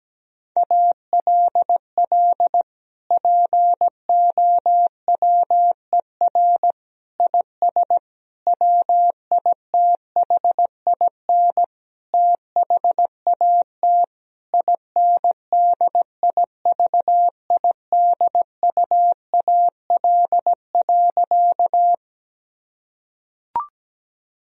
Morse code, heard as a keyed tone, about 17 wpm, 700 Hz. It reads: ALL POWER IS WITHIN THAT INDIVIDUAL.
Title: Sentences from Top 1000 Words - Review Part 4 - 17wpm